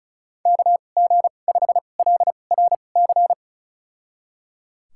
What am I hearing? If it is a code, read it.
Morse code, heard as a keyed tone, about 34 wpm, spaced as for 25 wpm, 700 Hz. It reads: KG5LRC
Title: USA Call Signs - 6 Characters - 2x Character Spacing - 35wpm